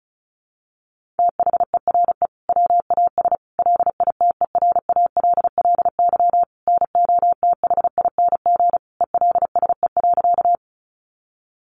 Morse code, heard as a keyed tone, about 34 wpm, 700 Hz. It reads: THERE WAS LITERALLY NOTHING ELSE.
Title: Sentences from Top 1400 Words 35wpm